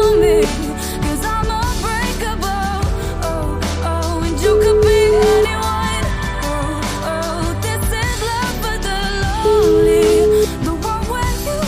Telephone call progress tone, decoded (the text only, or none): ringback tone